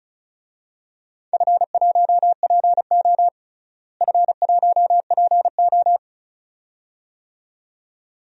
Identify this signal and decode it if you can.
Morse code, heard as a keyed tone, about 36 wpm, 690 Hz. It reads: F1PO F1PO
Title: DXCC-100 Call Signs - 4 Characters - Repeated 2x - 35wpm